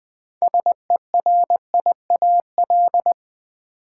Morse code, heard as a keyed tone, about 20 wpm, 690 Hz. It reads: SERIAL
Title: Build A Word - 2 Letters At A Time - 6-Letter Words - 20wpm